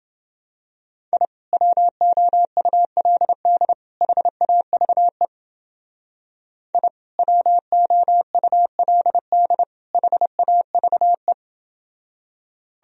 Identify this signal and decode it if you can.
Morse code, heard as a keyed tone, about 29 wpm, 700 Hz. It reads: I WOULD HAVE I WOULD HAVE